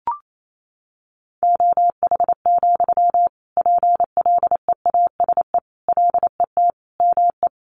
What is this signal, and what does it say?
Morse code, heard as a keyed tone, about 28 wpm, 700 Hz. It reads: OH, PLEASE LET ME
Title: Sentences from Top 1500 Words 28wpm